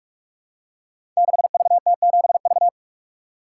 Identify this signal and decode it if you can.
Morse code, heard as a keyed tone, about 44 wpm, 700 Hz. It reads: 6VT7V
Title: Code Groups - 5 Letter:Numbers - 45wpm